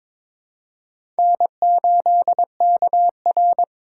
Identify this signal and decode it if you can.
Morse code, heard as a keyed tone, about 21 wpm, 700 Hz. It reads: N8KR
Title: USA Call Signs - 4 Characters - 2x Character Spacing - 22wpm